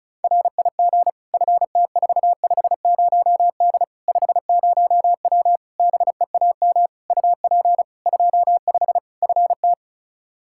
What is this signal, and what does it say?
Morse code, heard as a keyed tone, about 36 wpm, 700 Hz. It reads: RIG FT450D 50W BEAM UP 25 FT
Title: Practice QSO - Exchange 2 - Info - 35wpm